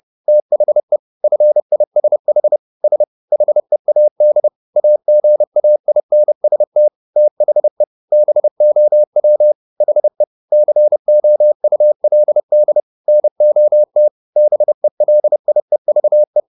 Morse code, heard as a keyed tone, about 30 wpm, 600 Hz. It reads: THE FISH S HEAD AGAINST THE BOW HE COULD NOT BELIEVE